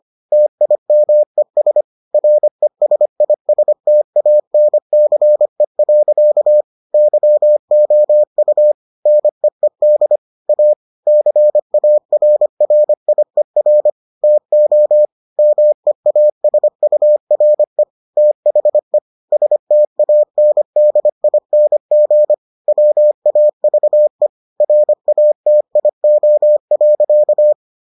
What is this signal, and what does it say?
Morse code, heard as a keyed tone, about 25 wpm, 600 Hz. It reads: TIMES RESISTANCE. YOU NEED A CARRIER TO MEASURE THE STANDING WAVE RATIO.